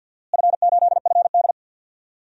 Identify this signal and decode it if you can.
Morse code, heard as a keyed tone, about 52 wpm, 700 Hz. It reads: F8FD
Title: DXCC-100 Call Signs - 4 Characters - Repeated 3x - 50wpm